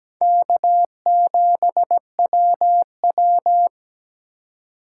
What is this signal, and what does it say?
Morse code, heard as a keyed tone, about 20 wpm, 700 Hz. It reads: K7WW
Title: Speed Racing - USA 1x2 Call Signs 17wpm